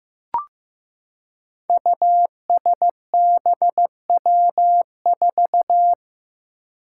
Morse code, heard as a keyed tone, about 15 wpm, 700 Hz. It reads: USBW4
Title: Five Challenging Letter-Number Combinations 15wpm